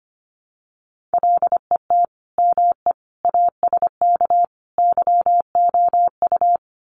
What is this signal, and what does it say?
Morse code, heard as a keyed tone, about 25 wpm, 700 Hz. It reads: LET ME ASK YOU
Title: Sets of 4 Words 25wpm